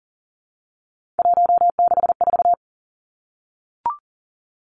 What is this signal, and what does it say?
Morse code, heard as a keyed tone, about 38 wpm, 700 Hz. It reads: 164
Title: Three Challenging Letter-Number Combinations 40wpm